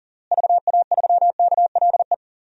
Morse code, heard as a keyed tone, about 40 wpm, 700 Hz. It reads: VA3KLE
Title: DXCC-100 Call Signs - 6 Characters - Repeated 3x - 40wpm